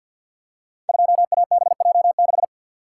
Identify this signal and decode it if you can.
Morse code, heard as a keyed tone, about 48 wpm, 700 Hz. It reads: 2ABJ6
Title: Code Groups - 5 Letter:Numbers - 50wpm